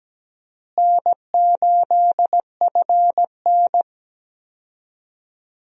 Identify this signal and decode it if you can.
Morse code, heard as a keyed tone, about 17 wpm, 700 Hz. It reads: N8FN
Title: USA Call Signs - 4 Characters - Repeated 2x - 17wpm